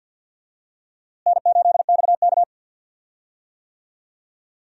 Morse code, heard as a keyed tone, about 52 wpm, 700 Hz. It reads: N8XK
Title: USA Call Signs - 4 Characters - Repeated 2x - 50wpm